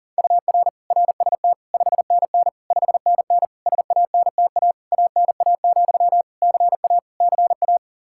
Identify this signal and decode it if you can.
Morse code, heard as a keyed tone, about 40 wpm, 700 Hz. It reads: UR RST 5NN 5NN SANTA ANA, CA CA
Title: Practice QSO - Exchange 1 - Beginning - 40wpm